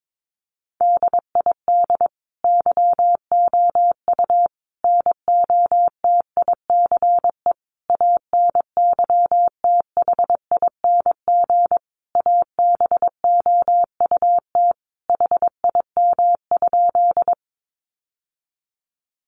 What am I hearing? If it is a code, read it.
Morse code, heard as a keyed tone, about 22 wpm, 700 Hz. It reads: DID YOU NOTICE ANYTHING ABOUT HIM?
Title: Sentences from Top 1000 Words 22wpm